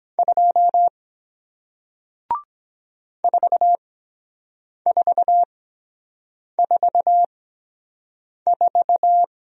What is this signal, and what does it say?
Morse code, heard as a keyed tone, about 23 wpm, spaced as for 7 wpm, 700 Hz. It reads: 2 4444